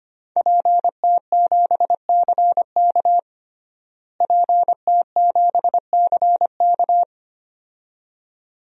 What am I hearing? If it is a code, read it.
Morse code, heard as a keyed tone, about 25 wpm, 700 Hz. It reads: PT7CK PT7CK
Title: DXCC-100 Call Signs - 5 Characters - Repeated 2x - 25wpm